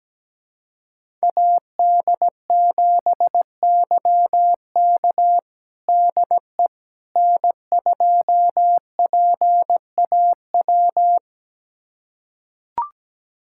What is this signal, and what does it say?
Morse code, heard as a keyed tone, about 17 wpm, 700 Hz. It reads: AD7YK DE N2PAW
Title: USA Callsign DE Callsign - 5 Characters - 17wpm